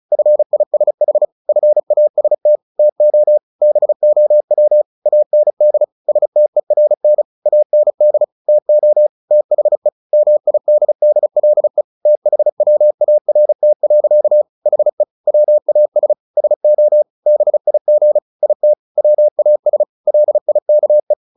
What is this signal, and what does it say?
Morse code, heard as a keyed tone, about 34 wpm, 600 Hz. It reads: FISH FAST TO BOW AND STERN AND TO THE MIDDLE THWART. HE WAS SO BIG IT WAS LIKE
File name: the old man and the sea 2 35wpm